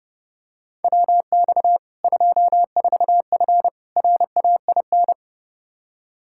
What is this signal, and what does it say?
Morse code, heard as a keyed tone, about 32 wpm, 700 Hz. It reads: WX 24F RAIN
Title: Practice QSO - Exchange 2 - Info - 30wpm